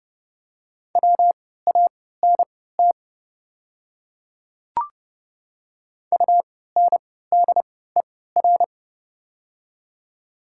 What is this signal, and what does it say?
Morse code, heard as a keyed tone, about 30 wpm, spaced as for 17 wpm, 700 Hz. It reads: WANT UNDER
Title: 3x Character Spacing - Top 200 Words - 30wpm